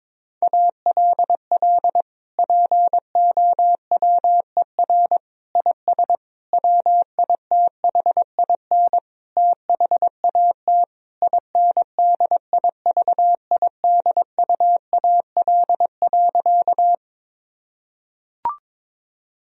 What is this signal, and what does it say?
Morse code, heard as a keyed tone, about 22 wpm, 700 Hz. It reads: ALL POWER IS WITHIN THAT INDIVIDUAL.
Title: Sentences from Top 1000 Words - Review Part 4 - 22wpm